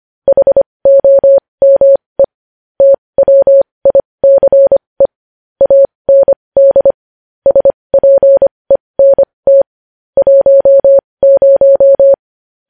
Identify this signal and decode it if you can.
Morse code, heard as a keyed tone, about 25 wpm, spaced as for 20 wpm, 550 Hz. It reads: HOME TWICE AND SPENT 10
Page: News Headlines in Morse Code at 20 WPM